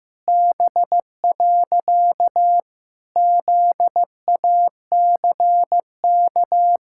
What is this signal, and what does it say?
Morse code, heard as a keyed tone, about 15 wpm, 700 Hz. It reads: B. ZACK
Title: Pangram Challenge v2 - 15wpm